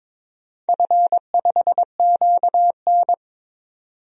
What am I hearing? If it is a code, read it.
Morse code, heard as a keyed tone, about 22 wpm, 700 Hz. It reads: F5QN
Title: DXCC-100 Call Signs - 4 Characters - 22wpm